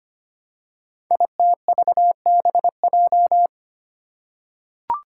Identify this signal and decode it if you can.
Morse code, heard as a keyed tone, about 25 wpm, 700 Hz. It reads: ITVBJ
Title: Code Groups - 5 Letters - 25wpm